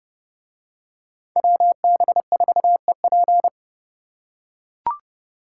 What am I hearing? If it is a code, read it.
Morse code, heard as a keyed tone, about 29 wpm, 700 Hz. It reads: WB4EP